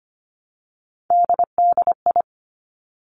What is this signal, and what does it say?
Morse code, heard as a keyed tone, about 25 wpm, 700 Hz. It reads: DDI
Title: Three Challenging Letter-Number Combinations 25wpm